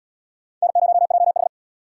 Morse code, heard as a keyed tone, about 73 wpm, 700 Hz. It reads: N0JH